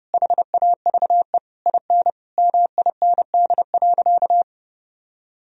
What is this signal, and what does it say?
Morse code, heard as a keyed tone, about 30 wpm, 700 Hz. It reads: HAVE IN MIND.
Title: Sentences from Top 1000 Words - Review Part 3 - 30wpm